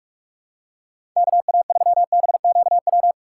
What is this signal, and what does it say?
Morse code, heard as a keyed tone, about 46 wpm, 700 Hz. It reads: KA3BQW
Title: USA Call Signs - 6 Characters - 6x Character Spacing - 45wpm